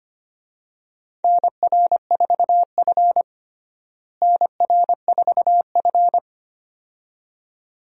Morse code, heard as a keyed tone, about 25 wpm, 700 Hz. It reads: NR4F NR4F
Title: USA Call Signs - 4 Characters - Repeated 2x - 25wpm